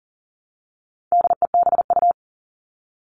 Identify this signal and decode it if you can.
Morse code, heard as a keyed tone, about 40 wpm, 700 Hz. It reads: DEBU
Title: All Cut Numbers - 4-Digit 40wpm